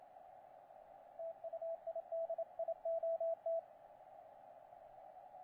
Morse code, heard as a keyed tone, about 28 wpm, 660 Hz, with radio noise. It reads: TU IDIOT
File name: XW7T, 28CW